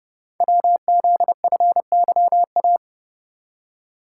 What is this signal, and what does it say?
Morse code, heard as a keyed tone, about 30 wpm, 700 Hz. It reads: WZFYA
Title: Code Groups - 5 Letters - 30wpm